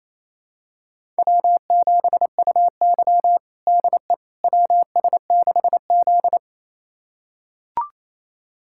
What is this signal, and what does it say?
Morse code, heard as a keyed tone, about 28 wpm, 700 Hz. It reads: W7UY DE WS6Z
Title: USA Callsign DE Callsign - 4 Characters - 28wpm